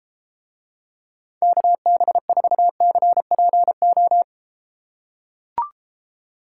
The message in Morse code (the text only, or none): KB4CPO